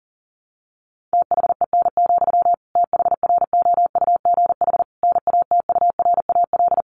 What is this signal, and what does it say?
Morse code, heard as a keyed tone, about 40 wpm, 700 Hz. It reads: THEN, THROUGH NATURAL
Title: Sentences from Top 1400 Words 40wpm